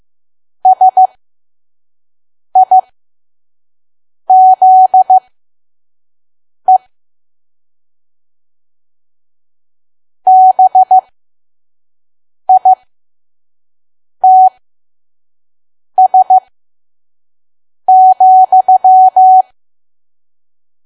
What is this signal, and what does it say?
Morse code, heard as a keyed tone, about 15 wpm, spaced as for 5 wpm, 750 Hz. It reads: SIZE BITS,
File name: Archive / 5 WPM files / PM